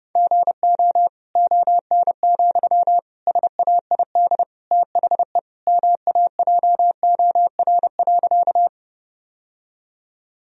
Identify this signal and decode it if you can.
Morse code, heard as a keyed tone, about 30 wpm, 700 Hz. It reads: GO ON, SAID THE MAJOR.
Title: Sentences from Top 1000 Words - Review Part 4 - 30wpm